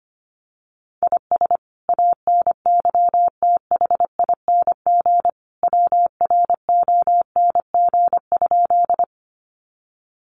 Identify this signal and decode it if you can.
Morse code, heard as a keyed tone, about 25 wpm, 700 Hz. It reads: IS ANYTHING WRONG?
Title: Sentences from Top 1000 Words 25wpm